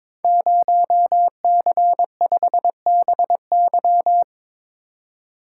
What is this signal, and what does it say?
Morse code, heard as a keyed tone, about 22 wpm, 700 Hz. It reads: 0C5BY
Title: Code Groups - 5 Letter:Numbers - 22wpm